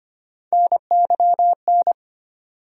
Morse code, heard as a keyed tone, about 25 wpm, 700 Hz. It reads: NYN
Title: USA Call Sign Postfixes - 3 Letters 25wpm